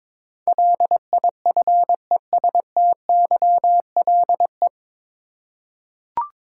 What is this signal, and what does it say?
Morse code, heard as a keyed tone, about 22 wpm, 700 Hz. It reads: LIFESTYLE
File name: 9 Letter Words - First 3 Letters Given - 22wpm